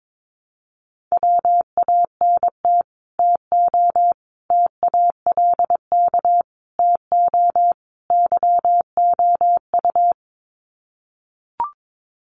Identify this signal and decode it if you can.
Morse code, heard as a keyed tone, about 22 wpm, 700 Hz. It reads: WANT TO TALK TO YOU
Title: Sets of 5 Words Encore 22wpm